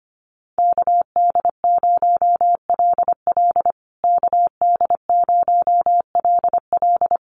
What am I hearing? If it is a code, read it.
Morse code, heard as a keyed tone, about 25 wpm, 700 Hz. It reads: KD0LL KD0LL
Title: USA Call Signs Repeated 3x 25wpm